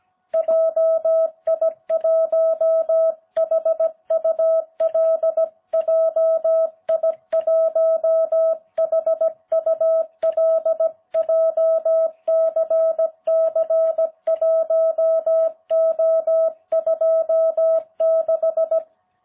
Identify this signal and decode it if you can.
Morse code, heard as a keyed tone, about 17 wpm, 660 Hz. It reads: JI1HUL JI1HUL JCC1O26